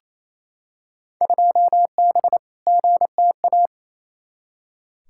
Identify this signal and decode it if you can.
Morse code, heard as a keyed tone, about 28 wpm, 700 Hz. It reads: 2B GTA
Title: Field Day Exchanges Encore - 28wpm